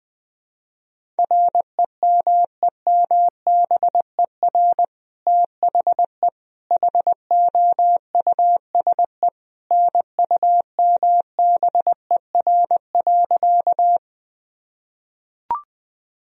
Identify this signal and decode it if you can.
Morse code, heard as a keyed tone, about 21 wpm, 700 Hz. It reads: REMEMBER THE HOUSE NUMBER.
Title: Sentences from Top 1000 Words - Review Part 1 - 20wpm